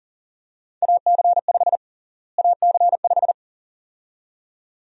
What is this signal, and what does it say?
Morse code, heard as a keyed tone, about 40 wpm, 700 Hz. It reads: AC5 AC5
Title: USA Call Sign Prefixes - Repeated 2x - 40wpm